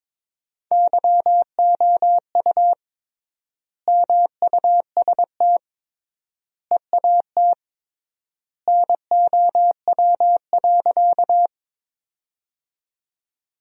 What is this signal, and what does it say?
Morse code, heard as a keyed tone, about 22 wpm, 700 Hz. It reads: YOU MUST EAT NOW.
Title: 3x Word-Spacing - Sentences from Top 300 Words - 22wpm